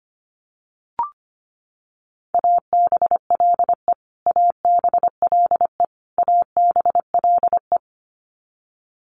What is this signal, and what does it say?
Morse code, heard as a keyed tone, about 25 wpm, 700 Hz. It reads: ABLE ABLE ABLE
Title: Top 500 Words Repeated 25wpm